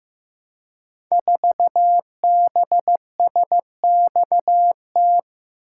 Morse code, heard as a keyed tone, about 15 wpm, 700 Hz. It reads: 4BSXT
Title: Code Groups - 5 Letter:Numbers - 15wpm